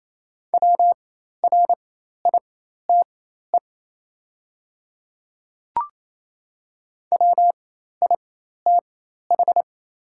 Morse code, heard as a keyed tone, about 28 wpm, spaced as for 13 wpm, 700 Hz. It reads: WRITE WITH
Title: 4x Character Spacing - Top 100 Words - 28wpm